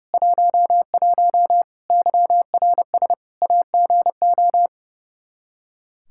Morse code, heard as a keyed tone, about 30 wpm, 700 Hz. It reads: 11 YRS AGO
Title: Practice QSO - Exchange 3 - More Info - 30wpm